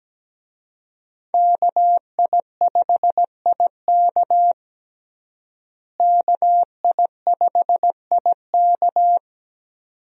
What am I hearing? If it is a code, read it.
Morse code, heard as a keyed tone, about 17 wpm, 700 Hz. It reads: KI5IK KI5IK
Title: USA Call Signs - 5 Characters - Repeated 2x - 17wpm